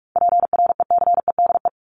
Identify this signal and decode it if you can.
Morse code, heard as a keyed tone, about 44 wpm, 700 Hz. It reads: PRECEDE
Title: ING Suffix - Top 100 Words with Base Words 45wpm